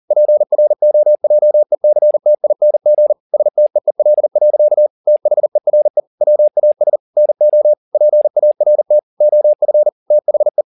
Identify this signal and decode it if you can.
Morse code, heard as a keyed tone, about 40 wpm, 600 Hz. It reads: PROJECTING STEEL. THERE WAS NO PART OF THE